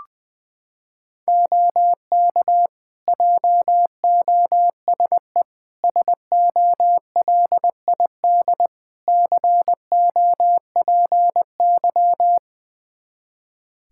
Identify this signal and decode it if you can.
Morse code, heard as a keyed tone, about 21 wpm, 700 Hz. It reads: OK JOSE SOLID COPY
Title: Practice QSO - Exchange 3 - More Info - 20wpm